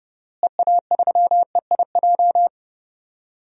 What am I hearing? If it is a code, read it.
Morse code, heard as a keyed tone, about 30 wpm, 700 Hz. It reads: EA3EIJ
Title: DXCC-100 Call Signs - 6 Characters - Repeated 3x - 30wpm